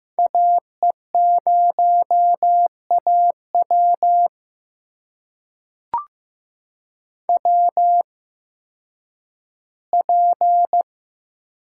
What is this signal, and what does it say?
Morse code, heard as a keyed tone, about 15 wpm, spaced as for 10 wpm, 700 Hz. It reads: AE0AW WP
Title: USA Call Signs - 5 Characters - 8x Character Spacing - 15wpm